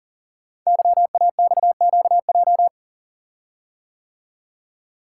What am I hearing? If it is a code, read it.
Morse code, heard as a keyed tone, about 41 wpm, 700 Hz. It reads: YAXQJ
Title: Code Groups - 5 Letters - 40wpm